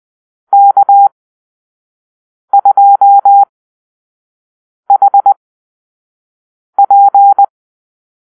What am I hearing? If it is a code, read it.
Morse code, heard as a keyed tone, about 20 wpm, spaced as for 5 wpm, 800 Hz. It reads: K2HP
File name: Call-Sign Drill